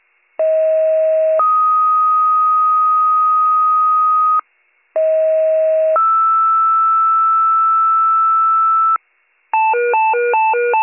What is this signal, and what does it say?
Two-tone page, heard as a one-second tone, then a three-second tone, about 640 then 1250 Hz, about 640 then 1320 Hz. It is heard from 0.4 to 9.0 s.